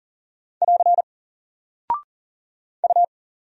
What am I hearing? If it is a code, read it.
Morse code, heard as a keyed tone, about 40 wpm, 700 Hz. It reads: <AR> U
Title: Single Character (Letter:Number:Punctuation:Prosign) 40wpm